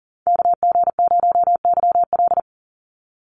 Morse code, heard as a keyed tone, about 40 wpm, 700 Hz. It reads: KG0YL
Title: Call Signs - 40wpm